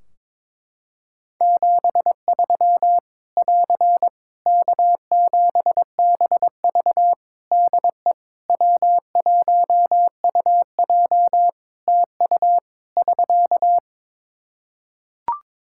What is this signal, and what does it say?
Morse code, heard as a keyed tone, about 22 wpm, 700 Hz. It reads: 73 <AR> K7BV DE W1UJ TU <SK>
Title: K1USN SST QSO Practice 22wpm